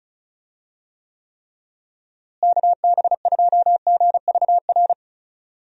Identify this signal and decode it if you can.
Morse code, heard as a keyed tone, about 34 wpm, 700 Hz. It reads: KB2GVR
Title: USA Call Signs - 6 Characters - Repeated 3x - 35wpm